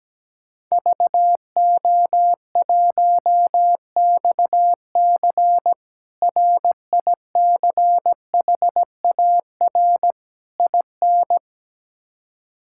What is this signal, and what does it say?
Morse code, heard as a keyed tone, about 17 wpm, 700 Hz. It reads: VO1XC RICHAR IN
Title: K1USN SST QSO Practice - 17wpm